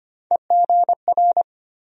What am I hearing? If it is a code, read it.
Morse code, heard as a keyed tone, about 25 wpm, 700 Hz. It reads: EGR
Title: USA Call Sign Postfixes - 3 Letters 25wpm